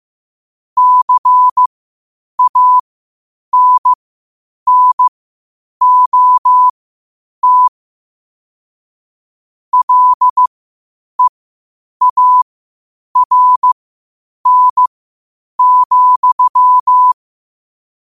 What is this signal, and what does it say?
Morse code, heard as a keyed tone, about 15 wpm, spaced as for 8 wpm, 1000 Hz. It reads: CANNOT LEARN,
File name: qotdfiles / 8wpm